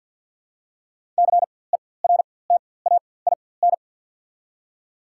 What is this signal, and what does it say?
Morse code, heard as a keyed tone, about 52 wpm, spaced as for 23 wpm, 690 Hz. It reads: CERTAIN
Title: 4x Character Spacing - Top 500 Words - 50wpm